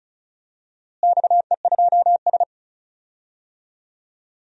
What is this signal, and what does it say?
Morse code, heard as a keyed tone, about 36 wpm, 700 Hz. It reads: XE2S
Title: DXCC-100 Call Signs - 4 Characters - Repeated 3x - 35wpm